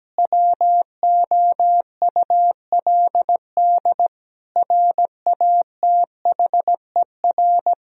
Morse code, heard as a keyed tone, about 17 wpm, 700 Hz. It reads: WOULD RATHER
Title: Sentences from Top 1000 Words - Review Part 4 - 17wpm